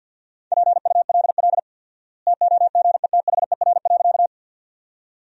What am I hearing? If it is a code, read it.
Morse code, heard as a keyed tone, about 49 wpm, 700 Hz. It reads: PULL TOGETHER.
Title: 4x Word-Spacing - Sentences from Top 400 Words - 50wpm